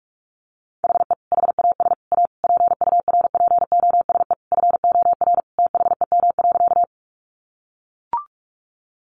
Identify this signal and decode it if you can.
Morse code, heard as a keyed tone, about 44 wpm, 700 Hz. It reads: HE HAS A PURPOSE FOR THEM.
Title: Sentences from Top 800 Words 45wpm